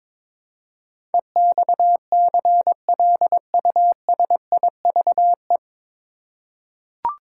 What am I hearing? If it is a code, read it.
Morse code, heard as a keyed tone, about 22 wpm, 700 Hz. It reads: EXCLUSIVE
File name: 9 Letter Words - First 3 Letters Given - 22wpm